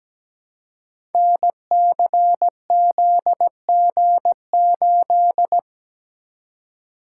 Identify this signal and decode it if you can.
Morse code, heard as a keyed tone, about 17 wpm, 700 Hz. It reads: NCZG8
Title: Code Groups - 5 Letter:Numbers - 17wpm